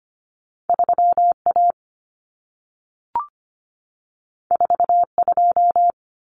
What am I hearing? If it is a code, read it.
Morse code, heard as a keyed tone, about 25 wpm, 700 Hz. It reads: 3A 42
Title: Common Cut Numbers - 2-Digit 25wpm